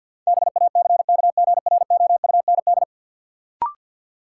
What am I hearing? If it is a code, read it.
Morse code, heard as a keyed tone, about 53 wpm, 700 Hz. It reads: BACKGROUND
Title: Sentences from Top 1300 Words - New Word Drill Alt - 50wpm